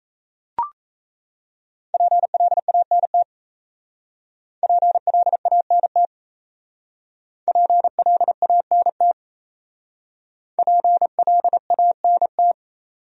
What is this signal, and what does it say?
Morse code, heard as a keyed tone, about 35 wpm, 690 Hz. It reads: PLANT PLANT PLANT PLANT